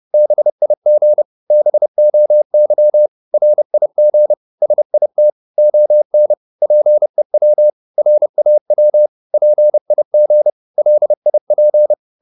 Morse code, heard as a keyed tone, about 30 wpm, 600 Hz. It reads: BIG BOY RIG SIT ON PEW RAW PIG LIP